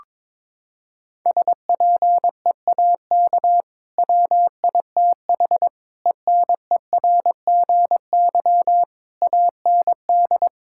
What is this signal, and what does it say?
Morse code, heard as a keyed tone, about 22 wpm, 700 Hz. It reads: SPEAK WITH ENERGY AND
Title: Sentences from Top 1000 Words - Review Part 6 - 22wpm